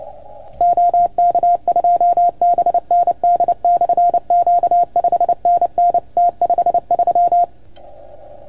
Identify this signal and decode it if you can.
Morse code, heard as a keyed tone, about 28 wpm, 690 Hz. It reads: OK2BND/Q 5NN T53
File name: Archiv / CUC310 / OK1FGD CUC 310 pileup